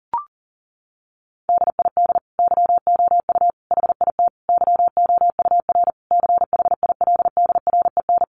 Morse code, heard as a keyed tone, about 37 wpm, 700 Hz. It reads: DID YOU HIT YOUR CHILDREN
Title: Sentences from Top 1400 Words 40wpm